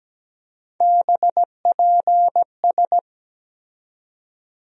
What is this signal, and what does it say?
Morse code, heard as a keyed tone, about 17 wpm, 700 Hz. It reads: BPS
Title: USA Call Sign Postfixes - 3 Letters 17wpm